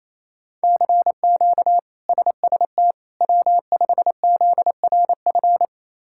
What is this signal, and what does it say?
Morse code, heard as a keyed tone, about 28 wpm, 700 Hz. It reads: CQ SST W5ZRF
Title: K1USN SST QSO Practice - 28wpm